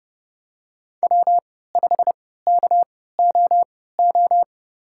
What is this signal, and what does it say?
Morse code, heard as a keyed tone, about 30 wpm, spaced as for 17 wpm, 700 Hz. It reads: W5KOO
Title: USA Call Signs - 5 Characters - 3x Character Spacing - 30wpm